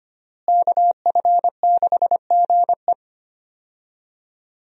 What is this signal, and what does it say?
Morse code, heard as a keyed tone, about 25 wpm, 700 Hz. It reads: KF6GE
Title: USA Call Signs - 5 Characters - Repeated 2x - 25wpm